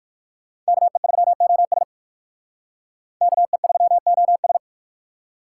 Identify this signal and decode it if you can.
Morse code, heard as a keyed tone, about 56 wpm, 700 Hz. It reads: KE3OS KE3OS